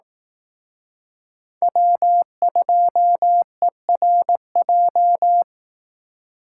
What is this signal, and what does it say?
Morse code, heard as a keyed tone, about 18 wpm, 700 Hz. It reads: W2ERJ
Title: Speed Racing - USA 1x3 Call Signs 15wpm